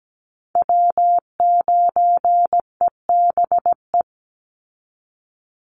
Morse code, heard as a keyed tone, about 17 wpm, 700 Hz. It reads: W9EBE
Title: Call Signs Repeated 3x - 17wpm